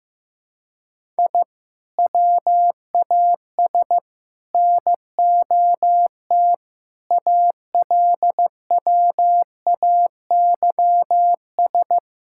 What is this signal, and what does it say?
Morse code, heard as a keyed tone, about 15 wpm, 700 Hz. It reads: I WAS NOT ALWAYS
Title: Sentences from Top 1000 Words - Review Part 6 - 15wpm